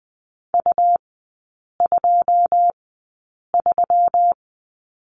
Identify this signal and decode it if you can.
Morse code, heard as a keyed tone, about 20 wpm, spaced as for 8 wpm, 690 Hz. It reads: U23
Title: 159 Lesson 26e - Letter g - Three Characters 20wpm